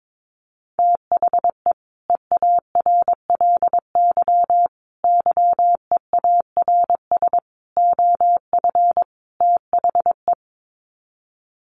Morse code, heard as a keyed tone, about 22 wpm, 700 Hz. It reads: THE EARLY YEARS OF THE
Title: Sets of 5 Words Encore 22wpm